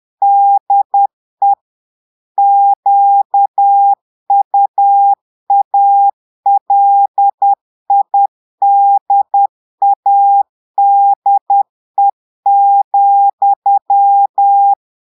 Morse code, heard as a keyed tone, about 10 wpm, 800 Hz. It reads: DE QUALIDADE,
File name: portugais texte en clair 2 vitesse 10 mots minutes